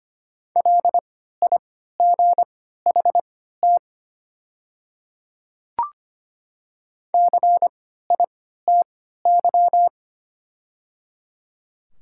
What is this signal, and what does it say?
Morse code, heard as a keyed tone, about 25 wpm, spaced as for 14 wpm, 700 Hz. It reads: LIGHT CITY
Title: 3x Character Spacing - Top 300 Words - 25wpm